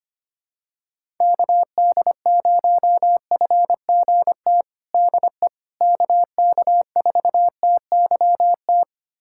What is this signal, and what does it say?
Morse code, heard as a keyed tone, about 25 wpm, 700 Hz. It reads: KD0FGT DE KK4TYT
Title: USA Callsign DE Callsign - 6 Characters - 25wpm